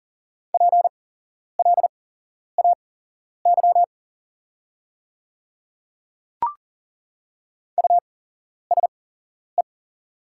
Morse code, heard as a keyed tone, about 37 wpm, spaced as for 11 wpm, 700 Hz. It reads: PLAY USE
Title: 8x Character Spacing - Top 200 Words - 40wpm